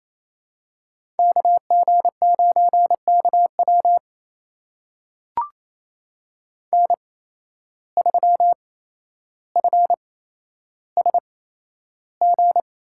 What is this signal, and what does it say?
Morse code, heard as a keyed tone, about 28 wpm, spaced as for 12 wpm, 700 Hz. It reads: KG9KW N3FSG
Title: USA Call Signs - 5 Characters - 8x Character Spacing - 28wpm